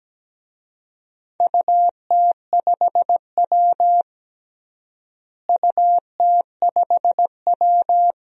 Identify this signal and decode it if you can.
Morse code, heard as a keyed tone, about 17 wpm, 700 Hz. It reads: UT5W UT5W
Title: DXCC-100 Call Signs - 4 Characters - Repeated 2x - 17wpm